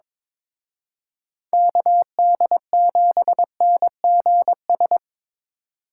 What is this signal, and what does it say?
Morse code, heard as a keyed tone, about 26 wpm, 700 Hz. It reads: KD7NGS
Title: Speed Racing - USA 2x3 Call Signs 22wpm